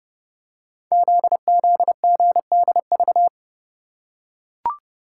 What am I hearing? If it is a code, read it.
Morse code, heard as a keyed tone, about 30 wpm, 700 Hz. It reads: ZZGDV
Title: Code Groups - 5 Letters - 30wpm